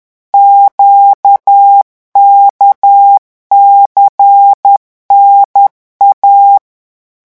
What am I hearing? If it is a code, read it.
Morse code, heard as a keyed tone, about 10 wpm, 800 Hz. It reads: QKCNA
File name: morse mp3 / revision5-vitesse 10 mots